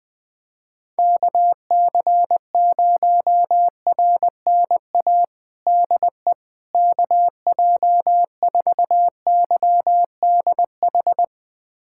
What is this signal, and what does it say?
Morse code, heard as a keyed tone, about 21 wpm, 700 Hz. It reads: KC0RNA DE KJ4YDH
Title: USA Callsign DE Callsign 20wpm